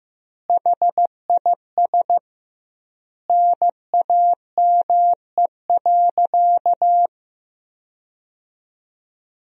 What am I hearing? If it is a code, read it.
Morse code, heard as a keyed tone, about 15 wpm, 700 Hz. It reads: HIS NAME.
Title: 2x Word-Spacing - Sentences from Top 200 Words - 15wpm